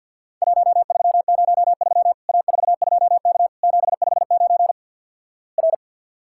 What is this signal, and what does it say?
Morse code, heard as a keyed tone, about 50 wpm, 700 Hz. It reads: 1303 A42K 759 R
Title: Worked All Europe DX Contest - QTC Exchange - 50wpm